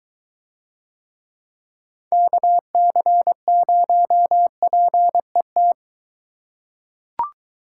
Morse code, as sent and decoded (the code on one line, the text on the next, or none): -.- -.-. ----- .--. -
KC0PET